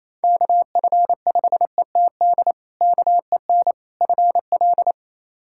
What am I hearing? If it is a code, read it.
Morse code, heard as a keyed tone, about 28 wpm, 700 Hz. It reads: KF5ETD KEN FL
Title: K1USN SST QSO Practice - 28wpm